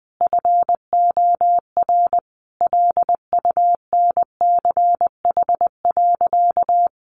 Morse code, heard as a keyed tone, about 19 wpm, 700 Hz. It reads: FOR LUNCH.
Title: Easy Phrases 20wpm